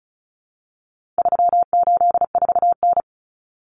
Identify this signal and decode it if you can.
Morse code, heard as a keyed tone, about 34 wpm, 700 Hz. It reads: 384N
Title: Common Cut Numbers - 4-Digit 35wpm